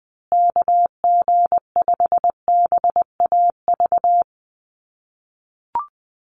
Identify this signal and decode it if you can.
Morse code, heard as a keyed tone, about 20 wpm, 700 Hz. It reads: KG5BAV
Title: USA Call Signs Repeated 2x 20wpm